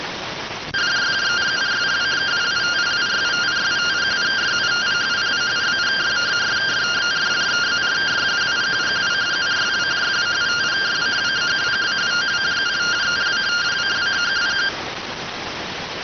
Two-tone teleprinter signal, about 170 Hz shift.